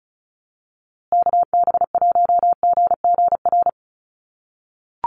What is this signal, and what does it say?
Morse code, heard as a keyed tone, about 34 wpm, 700 Hz. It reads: KB1GGR
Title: USA Call Signs Repeated 3x 35wpm